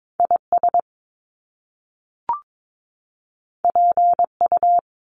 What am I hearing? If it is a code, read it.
Morse code, heard as a keyed tone, about 22 wpm, 700 Hz. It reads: IS PU
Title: Top Two Letter Combinations in Words 22wpm